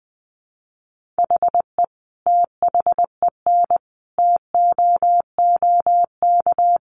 Morse code, heard as a keyed tone, about 19 wpm, 700 Hz. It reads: HE THEN TOOK
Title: Sentences from Top 1400 Words 20wpm